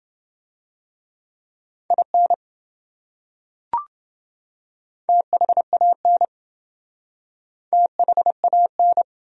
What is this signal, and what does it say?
Morse code, heard as a keyed tone, about 27 wpm, 700 Hz. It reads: IN THAN THAN